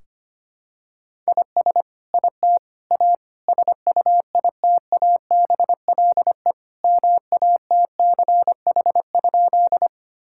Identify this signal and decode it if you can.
Morse code, heard as a keyed tone, about 25 wpm, 700 Hz. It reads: IS IT A SUITABLE MATCH?
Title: Sentences from Top 1600 Words 25wpm